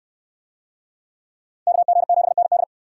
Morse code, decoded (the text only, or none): KG7ND